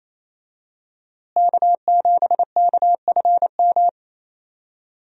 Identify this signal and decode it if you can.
Morse code, heard as a keyed tone, about 28 wpm, 700 Hz. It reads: K7KFM
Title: USA Call Signs - 5 Characters - Repeated 2x - 28wpm